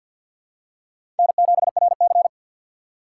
Morse code, heard as a keyed tone, about 51 wpm, 700 Hz. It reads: N7RC
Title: USA Call Signs - 4 Characters - 2x Character Spacing - 50wpm